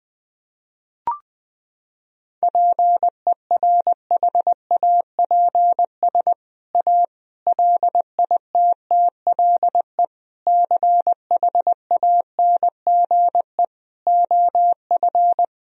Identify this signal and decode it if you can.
Morse code, heard as a keyed tone, about 21 wpm, 700 Hz. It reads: PERHAPS A LITTLE CHANGE OF
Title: Sentences from Top 1000 Words - Review Part 1 - 20wpm